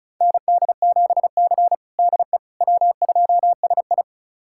Morse code, heard as a keyed tone, about 35 wpm, 700 Hz. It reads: ND7C DE W2SI
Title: USA Callsign DE Callsign - 4 Characters - 35wpm